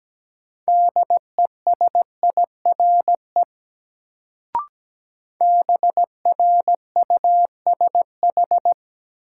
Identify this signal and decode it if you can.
Morse code, heard as a keyed tone, about 17 wpm, 700 Hz. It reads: DESIRE BRUSH
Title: Sentences from Top 1100 Words - New Word Drill - 17wpm